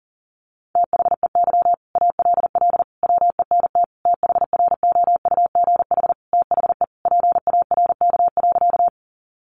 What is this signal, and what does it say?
Morse code, heard as a keyed tone, about 40 wpm, 700 Hz. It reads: THEY ALL WENT THROUGH THE PARK.
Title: Sentences from Top 1300 Words 40wpm